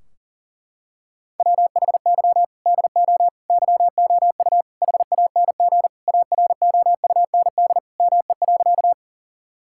Morse code, heard as a keyed tone, about 40 wpm, 700 Hz. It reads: WHY DO YOU HANG AROUND ME.
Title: Sentences from Top 1600 Words 40wpm